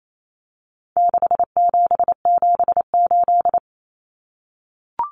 Morse code, heard as a keyed tone, about 28 wpm, 700 Hz. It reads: 6778
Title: Numbers - 4-Digit 28wpm